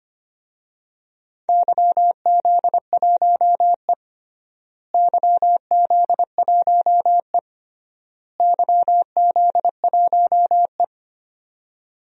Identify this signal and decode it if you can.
Morse code, heard as a keyed tone, about 25 wpm, 700 Hz. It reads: YZ1E YZ1E YZ1E
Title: DXCC-100 Call Signs - 4 Characters - Repeated 3x - 25wpm